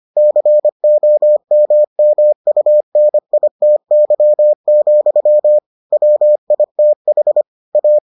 Morse code, heard as a keyed tone, about 25 wpm, 600 Hz. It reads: COMMUNITY, WITH A